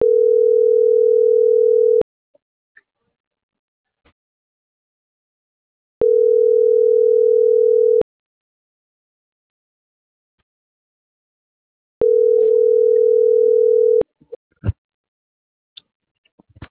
Ringback tone (North America), 440 + 480 Hz.